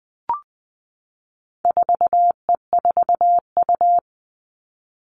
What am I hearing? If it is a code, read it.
Morse code, heard as a keyed tone, about 20 wpm, 700 Hz. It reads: 4E4U
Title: All Cut Numbers - 4-Digit 20wpm